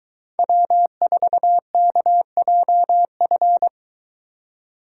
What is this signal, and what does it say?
Morse code, heard as a keyed tone, about 23 wpm, 700 Hz. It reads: W4KJF